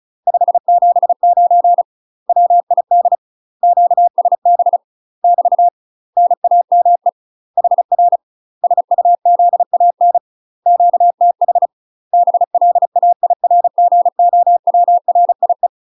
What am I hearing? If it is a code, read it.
Morse code, heard as a keyed tone, about 36 wpm, 700 Hz. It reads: 579 WID QSB <BT> NAME HR SUZAN QTH BLAIRGOWRIE
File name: UK MASTER QSOs TRAINING / 35WPM / UK MASTER QSOs 0033